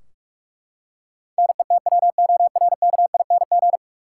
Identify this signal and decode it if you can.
Morse code, heard as a keyed tone, about 43 wpm, 700 Hz. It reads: NETWORKING
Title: Practice QSO - Exchange 3 - More Info - 45wpm